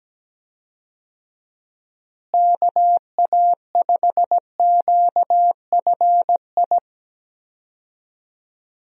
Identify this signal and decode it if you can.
Morse code, heard as a keyed tone, about 17 wpm, 700 Hz. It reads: KA5QFI
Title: USA Call Signs - 6 Characters - Repeated 3x - 17wpm